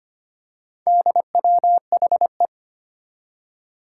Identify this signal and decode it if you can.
Morse code, heard as a keyed tone, about 25 wpm, 700 Hz. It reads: DWHE
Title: Four Challenging Letter-Number Combinations 25wpm